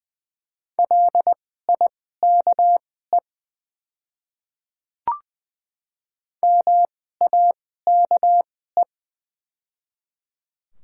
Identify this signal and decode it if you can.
Morse code, heard as a keyed tone, about 20 wpm, spaced as for 14 wpm, 700 Hz. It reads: LIKE MAKE
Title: 2x Character Spacing - Top 100 Words - 20wpm